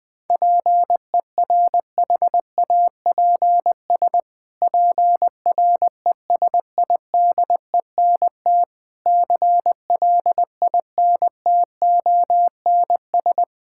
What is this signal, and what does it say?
Morse code, heard as a keyed tone, about 20 wpm, 700 Hz. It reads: PERHAPS PRESIDENT CLINTONS
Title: Pangram Challenge v2 - 20wpm